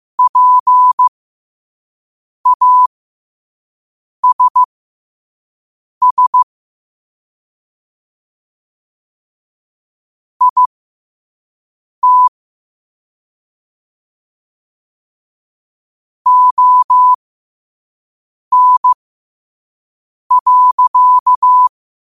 Morse code, heard as a keyed tone, about 15 wpm, spaced as for 5 wpm, 1000 Hz. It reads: PASS IT ON.
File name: qotdfiles / 5wpm